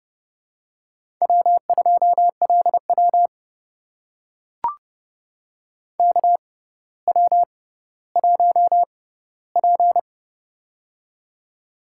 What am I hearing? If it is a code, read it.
Morse code, heard as a keyed tone, about 29 wpm, spaced as for 15 wpm, 700 Hz. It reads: W2LW KW1P
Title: USA Call Signs - 4 Characters - 6x Character Spacing - 30wpm